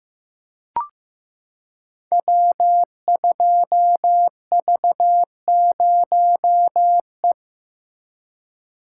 Morse code, heard as a keyed tone, about 15 wpm, 700 Hz. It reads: W2V0E
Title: Code Groups - 5 Letter:Numbers - 15wpm